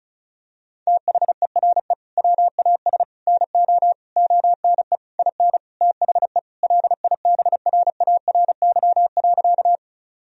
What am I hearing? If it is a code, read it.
Morse code, heard as a keyed tone, about 34 wpm, 700 Hz. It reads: THERE WAS NO ONE IN THE LIBRARY.
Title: Sentences from Top 1000 Words - Review Part 4 - 35wpm